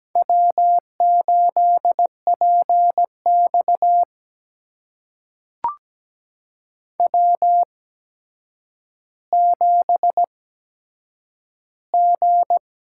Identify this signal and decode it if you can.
Morse code, heard as a keyed tone, about 17 wpm, spaced as for 8 wpm, 700 Hz. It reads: W8PX W7G
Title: USA Call Signs - 4 Characters - 8x Character Spacing - 17wpm